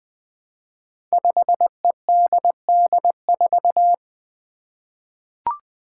Morse code, heard as a keyed tone, about 20 wpm, 700 Hz. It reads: 5EDD4
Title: Five Challenging Letter-Number Combinations 20wpm